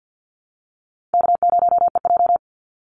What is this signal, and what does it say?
Morse code, heard as a keyed tone, about 50 wpm, 700 Hz. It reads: K0EJ
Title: Call Signs - 50wpm